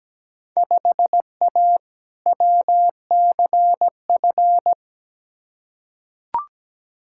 Morse code, heard as a keyed tone, about 17 wpm, 700 Hz. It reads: 5A WCF
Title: Field Day Exchanges Encore - 17wpm